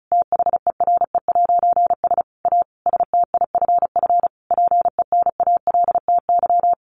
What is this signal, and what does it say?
Morse code, heard as a keyed tone, about 34 wpm, 700 Hz. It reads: THERE'S A STIFF PENALTY
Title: A Sound Of Thunder 35wpm